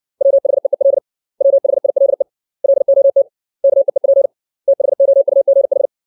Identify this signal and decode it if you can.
Morse code, heard as a keyed tone, about 60 wpm, 550 Hz. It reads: WHEEL WHILE LOT KEEP THOUGH
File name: 632 words / 632 Words 60WPM